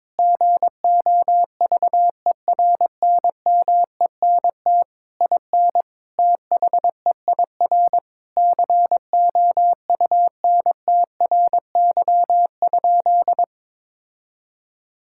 Morse code, heard as a keyed tone, about 22 wpm, 700 Hz. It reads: GOVERNMENT IN THEIR COUNTRY?
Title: Sentences from Top 1000 Words - Review Part 5 - 22wpm